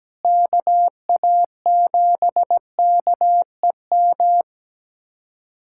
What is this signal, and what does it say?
Morse code, heard as a keyed tone, about 17 wpm, 700 Hz. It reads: KA7KEM
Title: USA Callsign DE Callsign - 6 Characters - 17wpm